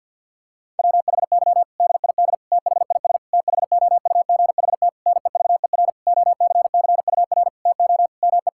Morse code, heard as a keyed tone, about 50 wpm, 700 Hz. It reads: WHY DID THIS THOUGHT NEVER OCCUR TO ME
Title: Sentences from Top 1000 Words - Review Part 6 - 50wpm